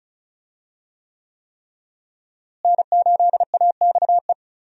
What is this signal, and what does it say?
Morse code, heard as a keyed tone, about 36 wpm, 700 Hz. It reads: N8AXE
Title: USA Call Signs - 5 Characters - Repeated 3x - 35wpm